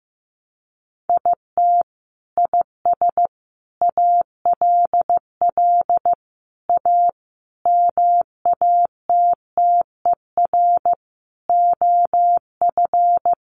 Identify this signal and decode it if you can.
Morse code, heard as a keyed tone, about 14 wpm, 700 Hz. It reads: IT IS ALL A MATTER OF
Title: Sentences from Top 1300 Words 15wpm